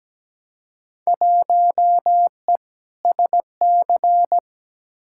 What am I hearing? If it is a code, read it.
Morse code, heard as a keyed tone, about 17 wpm, 700 Hz. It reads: 1E SC
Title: Field Day Exchanges - 17wpm